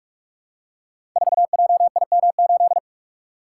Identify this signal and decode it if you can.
Morse code, heard as a keyed tone, about 44 wpm, 700 Hz. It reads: 4JIM8
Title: Code Groups - 5 Letter:Numbers - 45wpm